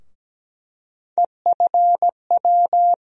Morse code, heard as a keyed tone, about 17 wpm, 700 Hz. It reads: EFW
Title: USA Call Sign Postfixes - 3 Letters 17wpm